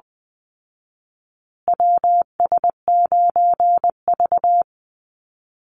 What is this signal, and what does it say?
Morse code, heard as a keyed tone, about 20 wpm, 700 Hz. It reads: WS9V
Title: Call Signs Repeated 3x - 20wpm